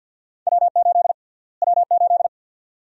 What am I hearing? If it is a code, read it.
Morse code, heard as a keyed tone, about 48 wpm, 700 Hz. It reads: W8 W8
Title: USA Call Sign Prefixes - Repeated 2x - 50wpm